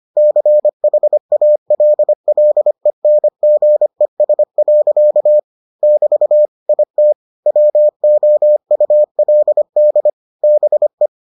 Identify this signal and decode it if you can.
Morse code, heard as a keyed tone, about 25 wpm, 600 Hz. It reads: CHALLENGES. <BT> IT WOULD BE